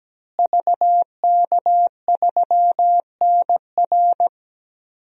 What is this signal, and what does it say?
Morse code, heard as a keyed tone, about 17 wpm, 700 Hz. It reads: VK3NR
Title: DXCC-100 Call Signs - 5 Characters - Repeated 3x - 17wpm